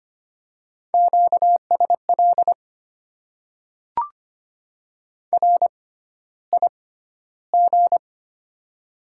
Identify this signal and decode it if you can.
Morse code, heard as a keyed tone, about 25 wpm, spaced as for 13 wpm, 700 Hz. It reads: QSL RIG
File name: Top 90 QSO Elements - 6x Character Spacing - 25wpm